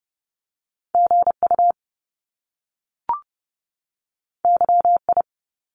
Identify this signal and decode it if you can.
Morse code, heard as a keyed tone, about 30 wpm, 700 Hz. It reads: G U Y I
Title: Top Two Letter Combinations in Words 30wpm